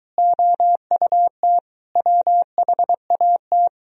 Morse code, heard as a keyed tone, about 23 wpm, 690 Hz. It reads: OUT WHAT